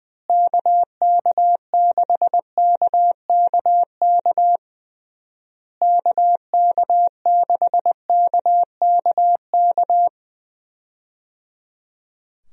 Morse code, heard as a keyed tone, about 21 wpm, 700 Hz. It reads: KK6KKK KK6KKK
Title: USA Call Signs - 6 Characters - Repeated 3x - 20wpm